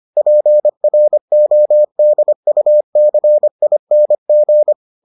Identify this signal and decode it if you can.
Morse code, heard as a keyed tone, about 25 wpm, 600 Hz. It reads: PRODUCING